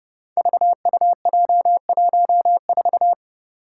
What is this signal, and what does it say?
Morse code, heard as a keyed tone, about 32 wpm, 700 Hz. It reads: VUJ14
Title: Five Challenging Letter-Number Combinations 30wpm